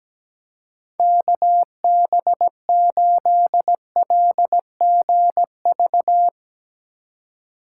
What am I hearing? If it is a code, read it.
Morse code, heard as a keyed tone, about 17 wpm, 700 Hz. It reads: KB8LGV
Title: USA Call Signs - 6 Characters - 4x Character Spacing - 17wpm